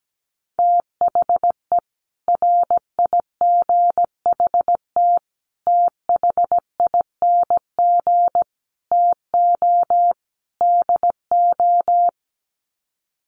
Morse code, heard as a keyed tone, about 17 wpm, 700 Hz. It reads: THE RIGHT THING TO DO
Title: Sets of 5 Words 17wpm